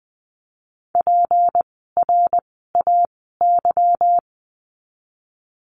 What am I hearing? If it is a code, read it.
Morse code, heard as a keyed tone, about 20 wpm, spaced as for 15 wpm, 700 Hz. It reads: PRAY
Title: 153 Lesson 25f - Letter p - Words 20wpm